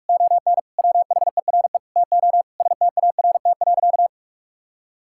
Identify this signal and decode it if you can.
Morse code, heard as a keyed tone, about 46 wpm, 700 Hz. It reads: ON WHERE TO START.
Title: Sentences from Top 1000 Words - Review Part 5 - 45wpm